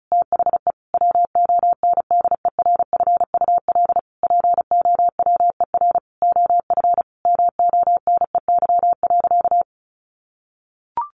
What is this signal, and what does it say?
Morse code, heard as a keyed tone, about 34 wpm, 700 Hz. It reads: THE WONDERFUL POWER OF MONEY.
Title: Sentences from Top 1100 Words 35wpm